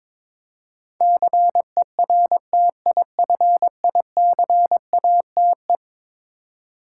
Morse code, heard as a keyed tone, about 22 wpm, 700 Hz. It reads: CERTIFICATE
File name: practice / 22wpm / Eleven Letter Words - 22wpm